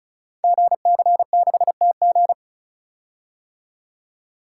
Morse code, heard as a keyed tone, about 35 wpm, 690 Hz. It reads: GC6TG
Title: Code Groups - 5 Letter:Numbers - 35wpm